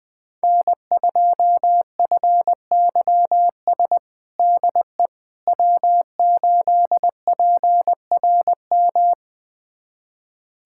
Morse code, heard as a keyed tone, about 20 wpm, 700 Hz. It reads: N2FYS DE W8PRM
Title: USA Callsign DE Callsign - 5 Characters - 20wpm